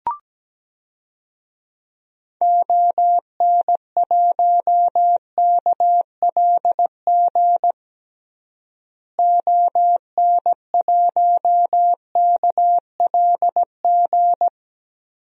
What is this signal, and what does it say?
Morse code, heard as a keyed tone, about 17 wpm, 700 Hz. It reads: ON1KLG ON1KLG
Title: DXCC-100 Call Signs - 6 Characters - Repeated 3x - 17wpm